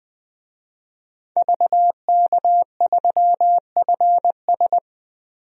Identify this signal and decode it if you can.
Morse code, heard as a keyed tone, about 21 wpm, 700 Hz. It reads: VK3FS
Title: USA Call Signs - 5 Characters - Repeated 2x - 20wpm